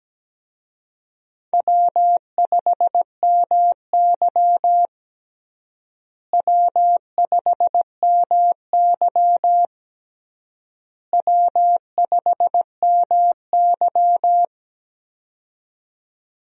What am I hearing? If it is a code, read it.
Morse code, heard as a keyed tone, about 17 wpm, 700 Hz. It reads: W5MY W5MY W5MY
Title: USA Call Signs - 4 Characters - Repeated 3x - 17wpm